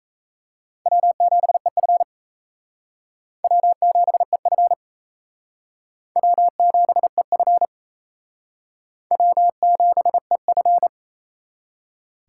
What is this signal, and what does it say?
Morse code, heard as a keyed tone, about 38 wpm, 700 Hz. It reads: W7EF W7EF W7EF W7EF